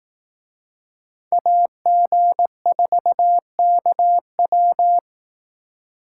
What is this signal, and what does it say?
Morse code, heard as a keyed tone, about 18 wpm, 700 Hz. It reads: AG4KW
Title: Speed Racing - USA Call Signs 15wpm